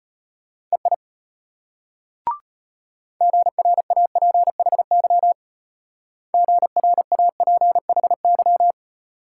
Morse code, heard as a keyed tone, about 32 wpm, 700 Hz. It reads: EI GRAPHY GRAPHY